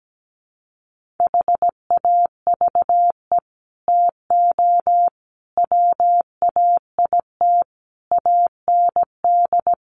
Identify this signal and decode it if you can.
Morse code, heard as a keyed tone, about 17 wpm, 700 Hz. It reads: HAVE TO WAIT AND
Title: Sets of 5 Words Encore2 17wpm